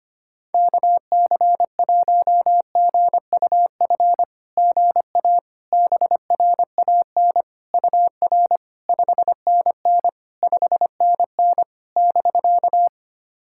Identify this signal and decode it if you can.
Morse code, heard as a keyed tone, about 25 wpm, 700 Hz. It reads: KC1GUF GA BRAN UR 5NN 5NN <BK>
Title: SOTA QSO Practice 25wpm